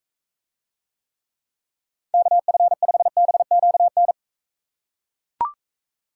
Morse code, heard as a keyed tone, about 43 wpm, 700 Hz. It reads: KF5BQN